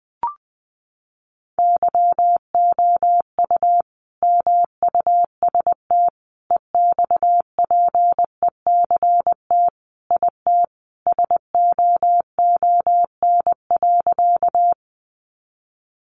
Morse code, heard as a keyed tone, about 20 wpm, 700 Hz. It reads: YOU MUST EXPECT IT SOON.
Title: Sentences from Top 1100 Words 20wpm